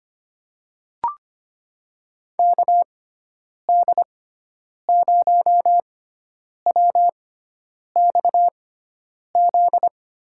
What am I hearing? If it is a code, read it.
Morse code, heard as a keyed tone, about 25 wpm, spaced as for 9 wpm, 700 Hz. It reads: KD0WXZ
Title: USA Call Signs - 6 Characters - 6x Character Spacing - 25wpm